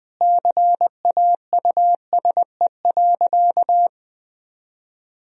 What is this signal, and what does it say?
Morse code, heard as a keyed tone, about 20 wpm, 700 Hz. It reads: CAUSE.
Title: Sentences from Top 1000 Words - Review Part 2 - 20wpm